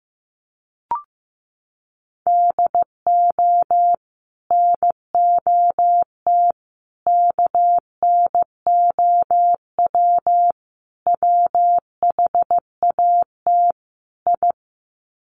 Morse code, heard as a keyed tone, about 15 wpm, 700 Hz. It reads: DO NOT KNOW WHAT I
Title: Sets of 5 Words 15wpm